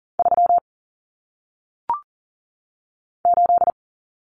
Morse code, heard as a keyed tone, about 40 wpm, 700 Hz. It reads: 3 8
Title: Common Cut Numbers - 1-Digit 40wpm